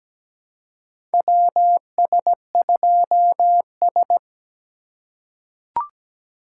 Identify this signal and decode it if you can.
Morse code, heard as a keyed tone, about 17 wpm, 700 Hz. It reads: WS2S
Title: USA Call Signs - 4 Characters - 4x Character Spacing - 17wpm